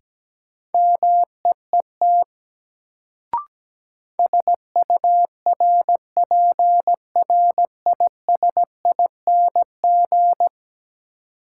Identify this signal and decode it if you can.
Morse code, heard as a keyed tone, about 17 wpm, 700 Hz. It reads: MEET SURPRISING
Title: Sets of 5 Words - New Word Drill - 17wpm